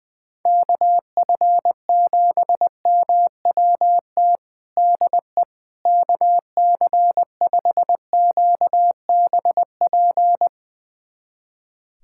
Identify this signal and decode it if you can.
Morse code, heard as a keyed tone, about 20 wpm, 700 Hz. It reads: KF7MWT DE KC5QBP
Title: USA Callsign DE Callsign - 6 Characters - 20wpm